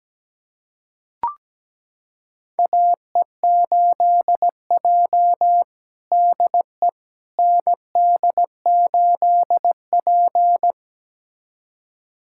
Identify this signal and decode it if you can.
Morse code, heard as a keyed tone, about 17 wpm, 700 Hz. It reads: AE8J DE ND8P
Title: USA Callsign DE Callsign - 4 Characters - 17wpm